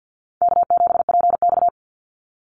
Morse code, heard as a keyed tone, about 50 wpm, 700 Hz. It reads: K7PX
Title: NA SOTA Activator Callsigns 50wpm